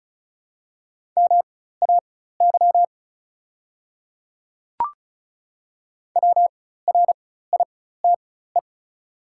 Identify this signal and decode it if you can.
Morse code, heard as a keyed tone, about 36 wpm, spaced as for 16 wpm, 700 Hz. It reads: MAY WRITE
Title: 4x Character Spacing - Top 100 Words - 35wpm